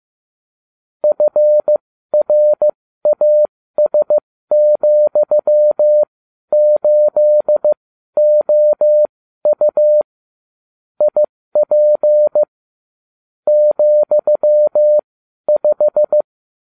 Morse code, heard as a keyed tone, about 15 wpm, spaced as for 12 wpm, 600 Hz. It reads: FRAS,8OU IP ,5